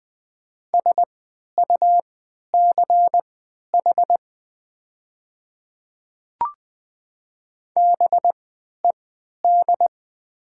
Morse code, heard as a keyed tone, about 20 wpm, spaced as for 11 wpm, 700 Hz. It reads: SUCH BED
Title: 3x Character Spacing - Top 500 Words - 20wpm